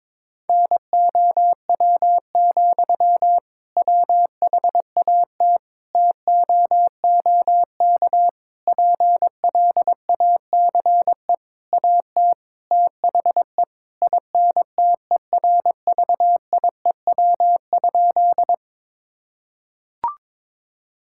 Morse code, heard as a keyed tone, about 22 wpm, 700 Hz. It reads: NOW, WHAT TOOK PLACE AT THE INTERVIEW?
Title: Sentences from Top 1600 Words 22wpm